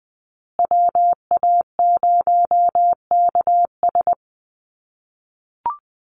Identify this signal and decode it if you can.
Morse code, heard as a keyed tone, about 20 wpm, 700 Hz. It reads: WA0KS
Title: USA Call Signs Repeated 3x 20wpm